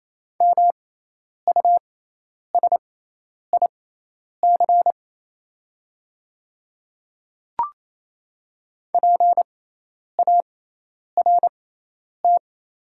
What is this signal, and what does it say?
Morse code, heard as a keyed tone, about 28 wpm, spaced as for 10 wpm, 690 Hz. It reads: MUSIC PART
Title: 6x Character Spacing - Top 500 Words - 28wpm